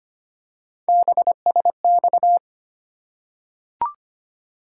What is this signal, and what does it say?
Morse code, heard as a keyed tone, about 25 wpm, 700 Hz. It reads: BSX
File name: Three Letter-Number Combinations - 25wpm